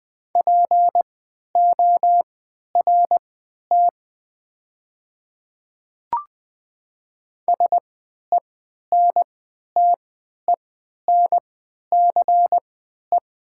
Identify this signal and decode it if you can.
Morse code, heard as a keyed tone, about 20 wpm, spaced as for 11 wpm, 690 Hz. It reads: PORT SENTENCE
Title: 3x Character Spacing - Top 300 Words - 20wpm